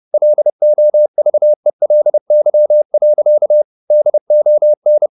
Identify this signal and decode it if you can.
Morse code, heard as a keyed tone, about 30 wpm, 600 Hz. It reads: LOVELY. DON
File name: the old man and the sea / the old man and the sea 1 30wpm